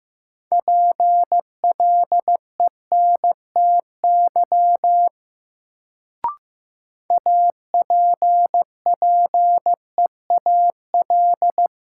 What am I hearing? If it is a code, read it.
Morse code, heard as a keyed tone, about 15 wpm, 700 Hz. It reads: PLENTY APPEAL
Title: Sentences from Top 1300 Words - New Word Drill Alt - 15wpm